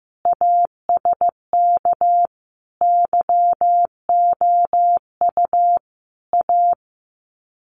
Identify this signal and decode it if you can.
Morse code, heard as a keyed tone, about 15 wpm, 700 Hz. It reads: ASK YOU A
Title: Sets of 5 Words Encore 15wpm